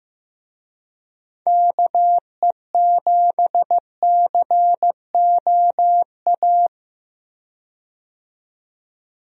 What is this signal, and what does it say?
Morse code, heard as a keyed tone, about 15 wpm, 700 Hz. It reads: KE7COA
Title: USA Call Signs - 6 Characters - 15wpm